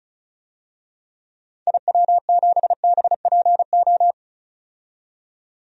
Morse code, heard as a keyed tone, about 34 wpm, 700 Hz. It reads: IW7BPO
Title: DXCC-100 Call Signs - 6 Characters - 35wpm